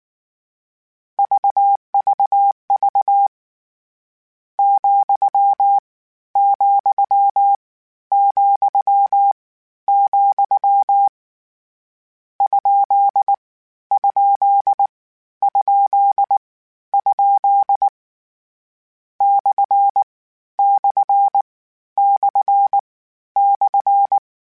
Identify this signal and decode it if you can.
Morse code, heard as a keyed tone, about 19 wpm, 800 Hz. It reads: VVV , , , , ? ? ? ? / / / /
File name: Misc / Misc-mp3s / Punct @18